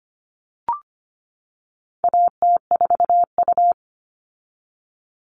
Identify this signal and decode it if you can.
Morse code, heard as a keyed tone, about 25 wpm, 700 Hz. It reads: AT4U
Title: All Cut Numbers - 4-Digit 25wpm